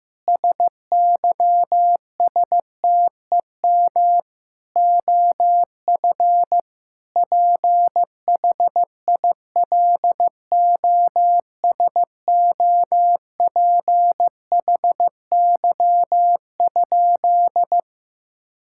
Morse code, heard as a keyed tone, about 15 wpm, 700 Hz. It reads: SYSTEM OF PHILOSOPHY?
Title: Sentences from Top 1000 Words - Review Part 6 - 15wpm